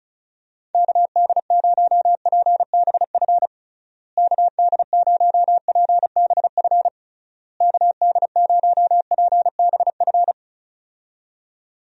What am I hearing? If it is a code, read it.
Morse code, heard as a keyed tone, about 34 wpm, 700 Hz. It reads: KD0PBF KD0PBF KD0PBF
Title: USA Call Signs - 6 Characters - Repeated 3x - 35wpm